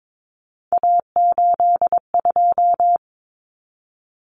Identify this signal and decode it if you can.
Morse code, heard as a keyed tone, about 22 wpm, 700 Hz. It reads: A82
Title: Common Cut Numbers - 3-Digit 22wpm